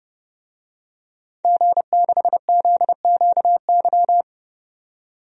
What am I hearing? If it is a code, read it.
Morse code, heard as a keyed tone, about 29 wpm, 700 Hz. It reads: G6ZQY
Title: DXCC-100 Call Signs - 5 Characters - 30wpm